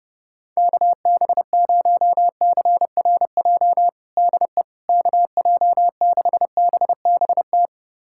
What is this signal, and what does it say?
Morse code, heard as a keyed tone, about 32 wpm, 700 Hz. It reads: KB0CRJ DE KJ6BBT
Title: USA Callsign DE Callsign - 6 Characters - 30wpm